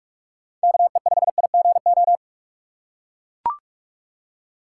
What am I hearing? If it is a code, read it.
Morse code, heard as a keyed tone, about 43 wpm, 700 Hz. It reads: KE5IGO
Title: USA Call Signs - 6 Characters - 3x Character Spacing - 45wpm